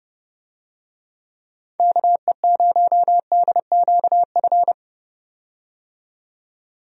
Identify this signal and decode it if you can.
Morse code, heard as a keyed tone, about 32 wpm, 700 Hz. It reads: KE0DQF
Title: USA Call Signs - 6 Characters - 30wpm